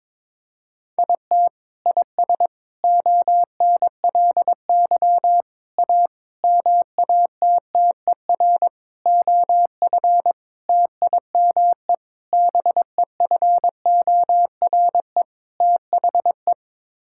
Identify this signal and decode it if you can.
Morse code, heard as a keyed tone, about 22 wpm, 700 Hz. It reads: IT IS ONLY A MATTER OF TIME BEFORE THE
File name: Sets of 9 Words - 22wpm